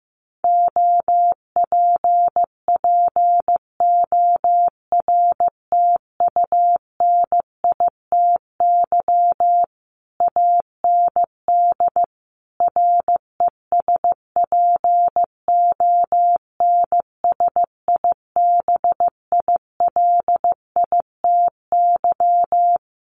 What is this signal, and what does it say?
Morse code, heard as a keyed tone, about 15 wpm, 700 Hz. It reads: OPPORTUNITY AND RESPONSIBILITY
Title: Sentences from Top 1200 Words 15wpm